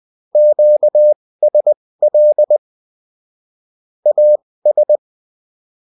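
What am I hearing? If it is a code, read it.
Morse code, heard as a keyed tone, about 20 wpm, spaced as for 16 wpm, 600 Hz. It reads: QSL AS
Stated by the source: UFT bulletin sent in CW.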